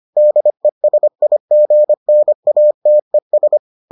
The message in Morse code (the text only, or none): DESIGNATES